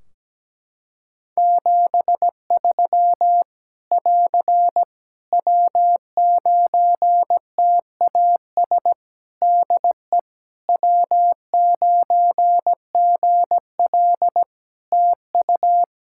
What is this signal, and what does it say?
Morse code, heard as a keyed tone, about 17 wpm, 700 Hz. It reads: 73 <AR> W9TAS DE W9GL TU
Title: Practice QSO - Exchange 4 - Ending - 17wpm